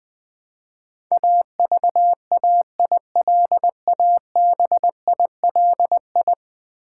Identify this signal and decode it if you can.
Morse code, heard as a keyed tone, about 20 wpm, 700 Hz. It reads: AVAILABILI
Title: Build A Word - 5 Letters At A Time - 12-Letter Words - 20wpm